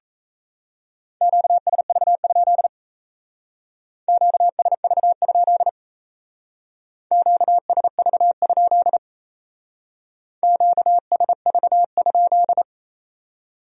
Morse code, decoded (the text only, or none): QSV? QSV? QSV? QSV?